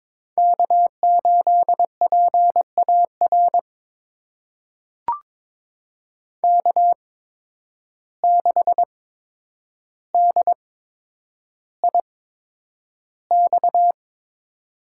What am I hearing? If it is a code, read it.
Morse code, heard as a keyed tone, about 21 wpm, spaced as for 9 wpm, 700 Hz. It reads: K8PAR K6DIX
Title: USA Call Signs - 5 Characters - 8x Character Spacing - 22wpm